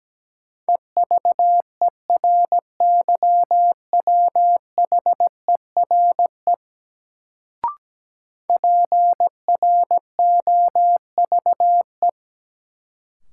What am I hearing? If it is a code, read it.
Morse code, heard as a keyed tone, about 17 wpm, 700 Hz. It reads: EVERYWHERE PROVE
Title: Sentences from Top 1600 Words - New Word Drill Alt - 17wpm